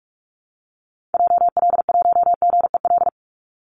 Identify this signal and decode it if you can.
Morse code, heard as a keyed tone, about 44 wpm, 700 Hz. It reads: JL1GEL
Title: Call Signs Repeated 3x - 45wpm